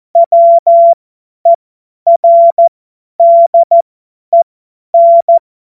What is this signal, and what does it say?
Morse code, heard as a keyed tone, about 14 wpm, spaced as for 10 wpm, 680 Hz. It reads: WERDEN
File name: Morsekurs / Witze / Witz 25. Mai 2021, 50 F70